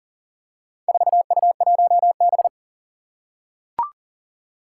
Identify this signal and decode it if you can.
Morse code, heard as a keyed tone, about 40 wpm, 700 Hz. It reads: 4U1B
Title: Four Challenging Letter-Number Combinations 40wpm